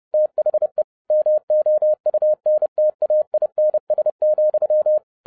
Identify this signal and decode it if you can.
Morse code, heard as a keyed tone, about 30 wpm, 610 Hz. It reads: THE MOUNTAINS,